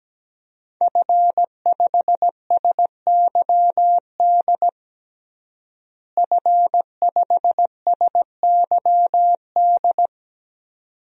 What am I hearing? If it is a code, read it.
Morse code, heard as a keyed tone, about 17 wpm, 700 Hz. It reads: F5SYD F5SYD
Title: DXCC-100 Call Signs - 5 Characters - Repeated 2x - 17wpm